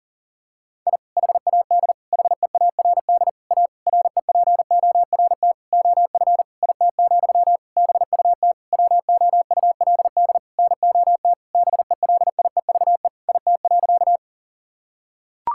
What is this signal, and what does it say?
Morse code, heard as a keyed tone, about 39 wpm, 700 Hz. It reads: I HAD HEARD A REPORT OF IT, BUT WOULD NOT BELIEVE IT.
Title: Sentences from Top 1000 Words - Review Part 6 - 40wpm